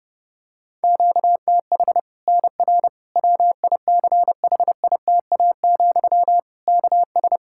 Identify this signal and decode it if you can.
Morse code, heard as a keyed tone, about 31 wpm, 700 Hz. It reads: QTH NR WICHITA, KS
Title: Practice QSO - Exchange 1 - Beginning - 30wpm